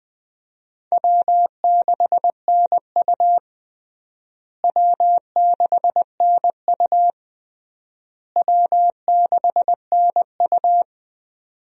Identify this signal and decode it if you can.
Morse code, heard as a keyed tone, about 20 wpm, 700 Hz. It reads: W6NU W6NU W6NU
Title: USA Call Signs - 4 Characters - Repeated 3x - 20wpm